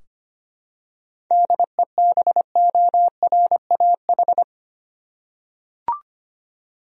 Morse code, heard as a keyed tone, about 25 wpm, 700 Hz. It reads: DEBORAH
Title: USA Common Names - 25wpm